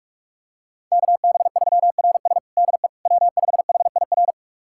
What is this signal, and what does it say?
Morse code, heard as a keyed tone, about 46 wpm, 700 Hz. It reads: KB3RS DE W5HIR
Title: USA Callsign DE Callsign 45wpm